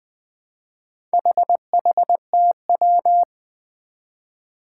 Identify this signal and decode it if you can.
Morse code, heard as a keyed tone, about 20 wpm, 700 Hz. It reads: HHTW
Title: Four Challenging Letter-Number Combinations 20wpm